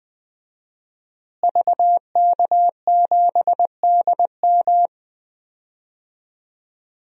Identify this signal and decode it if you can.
Morse code, heard as a keyed tone, about 20 wpm, 700 Hz. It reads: VK7DM
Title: DXCC-100 Call Signs - 5 Characters - Repeated 2x - 20wpm